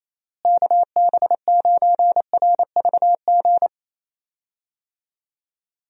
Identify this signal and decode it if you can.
Morse code, heard as a keyed tone, about 27 wpm, 700 Hz. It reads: KB9RVG
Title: USA Call Signs - 6 Characters - Repeated 2x - 28wpm